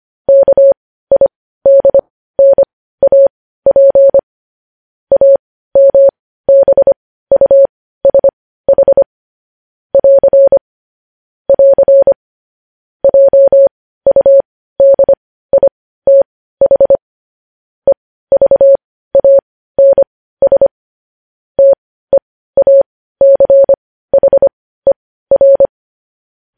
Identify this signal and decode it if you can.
Morse code, heard as a keyed tone, about 25 wpm, spaced as for 15 wpm, 550 Hz. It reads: KIDNAP AMBUSH <AR> <AR> JUDITH EVANS TEACHER
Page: news-headlines-in-morse-code-at-15-wpm